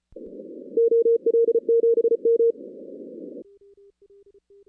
Morse code, heard as a keyed tone, about 34 wpm, 460 Hz. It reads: OL7M
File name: 2016 arrl dx cw / 2016-ARRL-DX-CW-K5TR- - 0-46